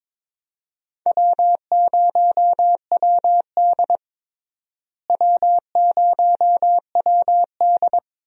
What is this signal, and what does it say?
Morse code, heard as a keyed tone, about 22 wpm, 700 Hz. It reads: W0WD W0WD
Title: USA Call Signs - 4 Characters - Repeated 2x - 22wpm